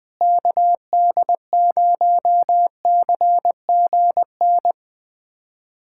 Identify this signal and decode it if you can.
Morse code, heard as a keyed tone, about 20 wpm, 700 Hz. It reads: KD0CGN
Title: USA Callsign DE Callsign - 6 Characters - 20wpm